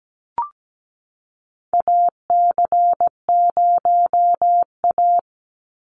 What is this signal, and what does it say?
Morse code, heard as a keyed tone, about 17 wpm, 700 Hz. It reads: AC0A
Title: NA SOTA Chaser Callsigns 17wpm